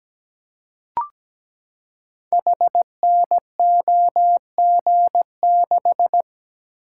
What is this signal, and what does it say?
Morse code, heard as a keyed tone, about 17 wpm, 700 Hz. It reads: HNOG6
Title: Code Groups - 5 Letter:Numbers - 17wpm